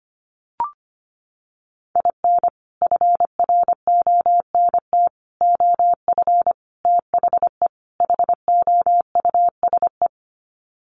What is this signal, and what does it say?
Morse code, heard as a keyed tone, about 25 wpm, 700 Hz. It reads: IN FRONT OF THE HOUSE
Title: Sets of 5 Words Encore2 25wpm